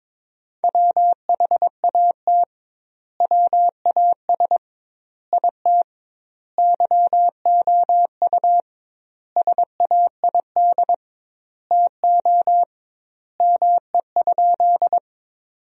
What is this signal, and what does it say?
Morse code, heard as a keyed tone, about 22 wpm, 700 Hz. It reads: WHAT WAS IT YOU SAID TO ME?
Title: 2x Word-Spacing - Sentences from Top 200 Words - 22wpm